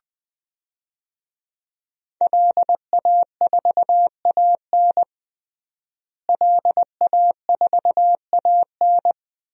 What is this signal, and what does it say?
Morse code, heard as a keyed tone, about 20 wpm, 700 Hz. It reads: LA4AN LA4AN
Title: DXCC-100 Call Signs - 5 Characters - Repeated 2x - 20wpm